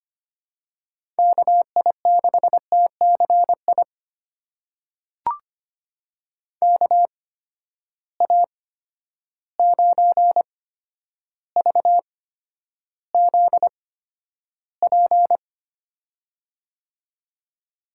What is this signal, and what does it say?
Morse code, heard as a keyed tone, about 25 wpm, spaced as for 11 wpm, 700 Hz. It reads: KI6TCI KA9VZP
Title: USA Call Signs - 6 Characters - 8x Character Spacing - 25wpm